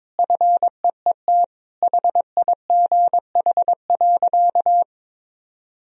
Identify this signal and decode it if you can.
Morse code, heard as a keyed tone, about 22 wpm, 700 Hz. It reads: FEET HIGH.
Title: Sentences from Top 1000 Words - Review Part 2 - 22wpm